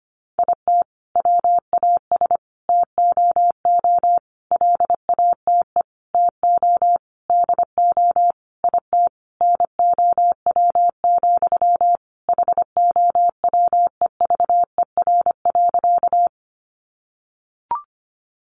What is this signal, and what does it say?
Morse code, heard as a keyed tone, about 25 wpm, 700 Hz. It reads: IT WAS TOO LATE TO DO IT NOW, HOWEVER.
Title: Sentences from Top 600 Words 25wpm